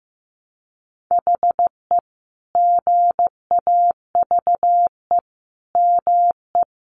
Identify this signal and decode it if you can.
Morse code, heard as a keyed tone, about 14 wpm, 700 Hz. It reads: HE GAVE ME
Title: Sentences from Top 500 Words 15wpm